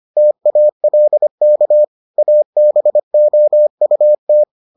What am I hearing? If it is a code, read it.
Morse code, heard as a keyed tone, about 25 wpm, 600 Hz. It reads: TALK ABOUT